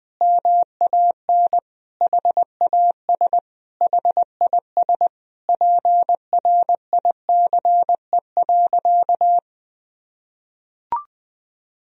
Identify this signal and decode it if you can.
Morse code, heard as a keyed tone, about 20 wpm, 700 Hz. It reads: MAN HAS HIS PRICE.
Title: Sentences from Top 1000 Words - Review Part 4 - 20wpm